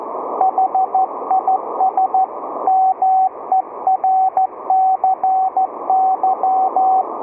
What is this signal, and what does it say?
Morse code, heard as a keyed tone, about 14 wpm, 760 Hz, with radio noise. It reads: HIS MERCY